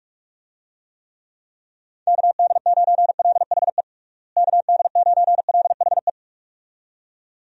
Morse code, heard as a keyed tone, about 46 wpm, 700 Hz. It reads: KD9LHE KD9LHE
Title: USA Call Signs - 6 Characters - Repeated 2x - 45wpm